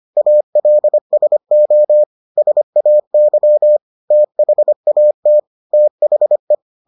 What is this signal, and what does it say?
Morse code, heard as a keyed tone, about 25 wpm, 600 Hz. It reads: ALSO SAY THAT THE